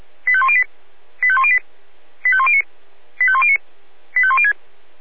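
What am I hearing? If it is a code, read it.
Five-tone selective call (ZVEI), heard as quick runs of five tones, tones of about 70 ms. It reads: 85108 85108 85109 85109 85106
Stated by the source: ZVEI five-tone selective call received by SDR.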